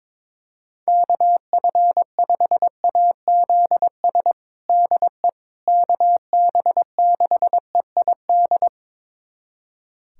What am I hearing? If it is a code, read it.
Morse code, heard as a keyed tone, about 22 wpm, 700 Hz. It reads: KF5AZS DE KB6EID
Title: USA Callsign DE Callsign - 6 Characters - 22wpm